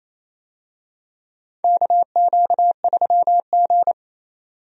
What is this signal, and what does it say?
Morse code, heard as a keyed tone, about 27 wpm, 700 Hz. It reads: KQ3G
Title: USA Call Signs - 4 Characters - 28wpm